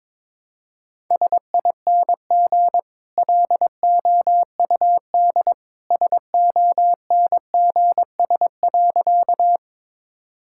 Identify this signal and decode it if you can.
Morse code, heard as a keyed tone, about 22 wpm, 700 Hz. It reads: SING LOUD SONGS.
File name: practice / 22wpm / Easy Sentences with Hints - 22wpm